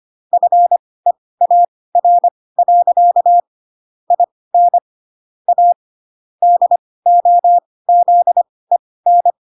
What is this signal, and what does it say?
Morse code, heard as a keyed tone, about 25 wpm, spaced as for 18 wpm, 700 Hz. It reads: FEAR. IN A DOZEN
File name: Pinocchio / Pinocchio ch6 12 25 ik1hge